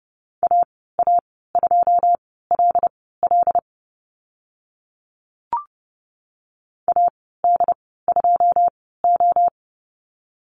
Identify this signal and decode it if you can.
Morse code, heard as a keyed tone, about 32 wpm, spaced as for 17 wpm, 700 Hz. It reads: AA2LL AD2O
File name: Lesson 14g - Number 2 - Call Signs 30wpm